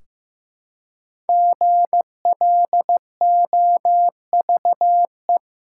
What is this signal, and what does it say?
Morse code, heard as a keyed tone, about 15 wpm, 700 Hz. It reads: GLOVE
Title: Five Letter Words - 15wpm